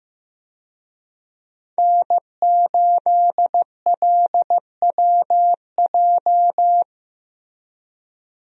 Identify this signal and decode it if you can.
Morse code, heard as a keyed tone, about 15 wpm, 700 Hz. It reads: N8LWJ
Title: USA Call Signs - 5 Characters - Repeated 3x - 15wpm